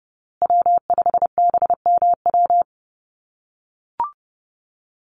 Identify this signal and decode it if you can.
Morse code, heard as a keyed tone, about 30 wpm, 700 Hz. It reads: W5BMW
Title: USA Call Signs Repeated 2x 30wpm